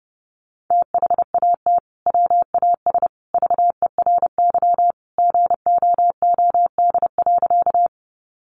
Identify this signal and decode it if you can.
Morse code, heard as a keyed tone, about 30 wpm, 700 Hz. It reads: THAT WAS VERY GOOD.
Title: Sentences from Top 200 Words 30wpm